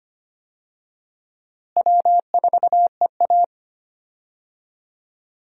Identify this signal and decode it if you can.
Morse code, heard as a keyed tone, about 25 wpm, 700 Hz. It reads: W4EA
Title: USA Call Signs - 4 Characters - Repeated 3x - 25wpm